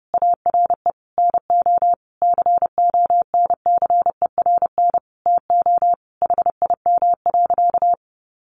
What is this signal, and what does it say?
Morse code, heard as a keyed tone, about 30 wpm, 700 Hz. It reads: ARE NO CONCERN TO HIM.
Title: Sentences from Top 1200 Words 30wpm